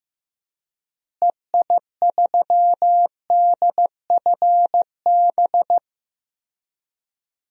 Morse code, heard as a keyed tone, about 15 wpm, 700 Hz. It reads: EI3DFB
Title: DXCC-100 Call Signs - 6 Characters - Repeated 3x - 15wpm